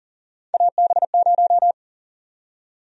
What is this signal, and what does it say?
Morse code, heard as a keyed tone, about 39 wpm, 700 Hz. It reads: AB0
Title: USA Call Sign Prefixes - 40wpm